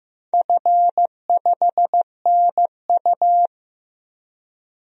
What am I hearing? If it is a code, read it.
Morse code, heard as a keyed tone, about 15 wpm, 700 Hz. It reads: F5NU
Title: DXCC-100 Call Signs - 4 Characters - 15wpm